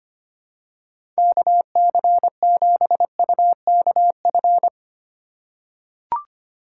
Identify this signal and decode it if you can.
Morse code, heard as a keyed tone, about 25 wpm, 700 Hz. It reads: KC7UKF
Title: USA Call Signs - 6 Characters - 4x Character Spacing - 25wpm